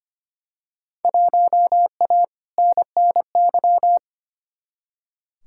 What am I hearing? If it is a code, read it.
Morse code, heard as a keyed tone, about 25 wpm, 700 Hz. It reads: 1A NNY
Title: Field Day Exchanges - 25wpm